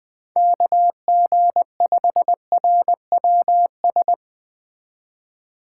Morse code, heard as a keyed tone, about 20 wpm, 700 Hz. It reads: KG5RWS
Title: Speed Racing - USA 2x3 Call Signs 17wpm